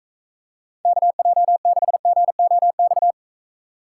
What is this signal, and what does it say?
Morse code, heard as a keyed tone, about 44 wpm, 700 Hz. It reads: KJ6GOX